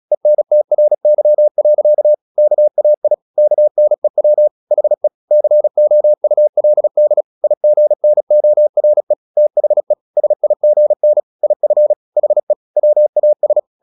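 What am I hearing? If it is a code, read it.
Morse code, heard as a keyed tone, about 36 wpm, 600 Hz. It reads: ENTRY. KAI KNEW HE COULD IGNORE THE SIGN IF HE WAS
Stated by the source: Morse practice at 36 wpm, text of Ken Follett's novel Never, from CW med Gnister.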